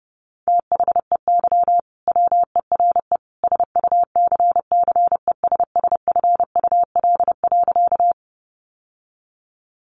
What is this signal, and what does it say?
Morse code, heard as a keyed tone, about 30 wpm, 700 Hz. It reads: THEY WERE SUCCESSFUL.
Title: Sentences from Top 900 Words 30wpm